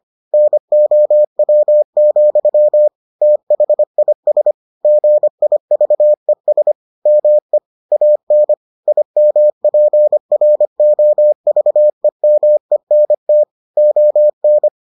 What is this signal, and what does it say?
Morse code, heard as a keyed tone, about 25 wpm, 600 Hz. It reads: NOW, THIS GIVES ME AN IMPROVEMENT ON